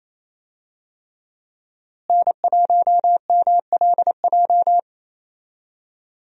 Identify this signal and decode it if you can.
Morse code, heard as a keyed tone, about 27 wpm, 700 Hz. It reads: N1MLJ
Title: USA Call Signs - 5 Characters - Repeated 3x - 28wpm